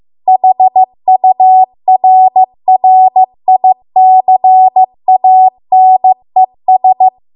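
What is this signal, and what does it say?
Morse code, heard as a keyed tone, about 15 wpm, 750 Hz. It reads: HURRICANES
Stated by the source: Morse code practice at 15 wpm from the national amateur radio association's code archive.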